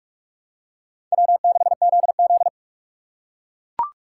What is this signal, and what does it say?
Morse code, heard as a keyed tone, about 40 wpm, 700 Hz. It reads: W6ZZ